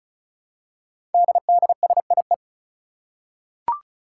Morse code, heard as a keyed tone, about 35 wpm, 700 Hz. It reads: DDSIE
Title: Five Challenging Letter-Number Combinations 35wpm